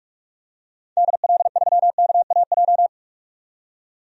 Morse code, heard as a keyed tone, about 43 wpm, 700 Hz. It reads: DL3KAJ
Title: DXCC-100 Call Signs - 6 Characters - 45wpm